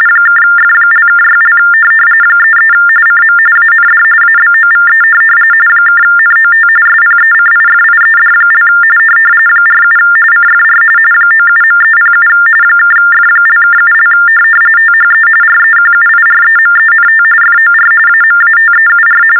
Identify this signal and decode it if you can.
TTY/TDD text telephone tones, about 45.45 baud. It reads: (F)(6) 33 CM AND ABOVE (NOT SPECIFIED) 47 CFR 97.307 (F)(7) THE REQUIREMENT FOR AMATEUR
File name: teletype-RTTY-45